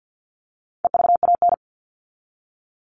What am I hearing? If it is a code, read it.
Morse code, heard as a keyed tone, about 50 wpm, 700 Hz. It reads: EVAN
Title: All Cut Numbers - 4-Digit 50wpm